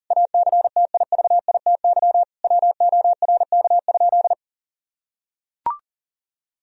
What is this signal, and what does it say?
Morse code, heard as a keyed tone, about 37 wpm, 700 Hz. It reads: ACTIVITY WORK?
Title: Sentences from Top 1000 Words - Review Part 4 - 40wpm